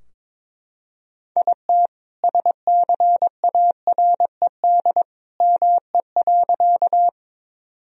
Morse code, heard as a keyed tone, about 21 wpm, 700 Hz. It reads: IT SCARED ME.
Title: Sentences from Top 1000 Words - Review Part 4 - 22wpm